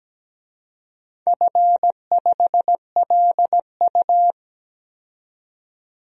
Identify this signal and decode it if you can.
Morse code, heard as a keyed tone, about 17 wpm, 700 Hz. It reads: F5LU
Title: DXCC-100 Call Signs - 4 Characters - 17wpm